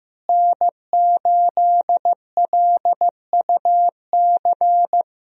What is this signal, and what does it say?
Morse code, heard as a keyed tone, about 15 wpm, 700 Hz. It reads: N8LUC
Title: USA Call Signs - 5 Characters - Repeated 2x - 15wpm